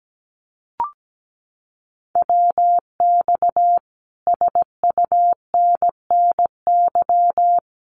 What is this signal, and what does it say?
Morse code, heard as a keyed tone, about 17 wpm, 700 Hz. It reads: WX SUNNY
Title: Common QSO Phrases Encore 17wpm